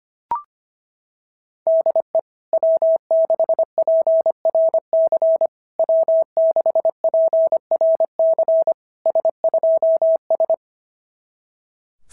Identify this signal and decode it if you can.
Morse code, heard as a keyed tone, about 24 wpm, 650 Hz. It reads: DE W6PRC W6PRC S2S
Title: SOTA QSO Practice 25wpm